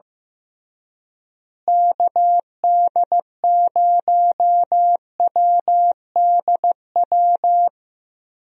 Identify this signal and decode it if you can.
Morse code, heard as a keyed tone, about 15 wpm, 700 Hz. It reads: KD0WDW
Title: USA Call Signs - 6 Characters - Repeated 3x - 15wpm